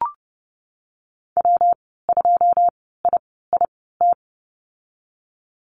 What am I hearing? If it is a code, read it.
Morse code, heard as a keyed tone, about 31 wpm, spaced as for 17 wpm, 700 Hz. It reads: W2IIT
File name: Lesson 19g - Letter w - Call Signs 30wpm